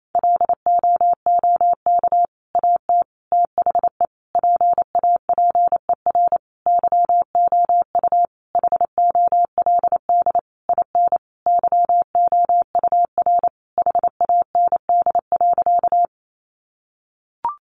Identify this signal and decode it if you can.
Morse code, heard as a keyed tone, about 28 wpm, 700 Hz. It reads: LOOK AT THE PAPER YOU HOLD IN YOUR HAND.
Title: Sentences from Top 400 Words 28wpm